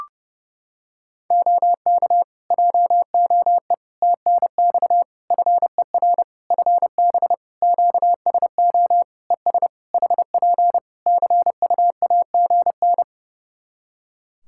Morse code, heard as a keyed tone, about 30 wpm, 700 Hz. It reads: OK JOE TNX FER FB QSO ES HP CUAGN
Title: Practice QSO - Exchange 4 - Ending - 30wpm